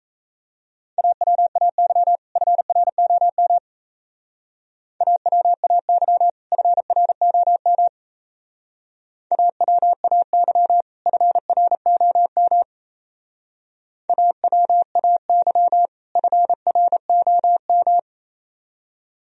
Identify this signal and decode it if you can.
Morse code, heard as a keyed tone, about 35 wpm, 690 Hz. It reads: AWAY FROM AWAY FROM AWAY FROM AWAY FROM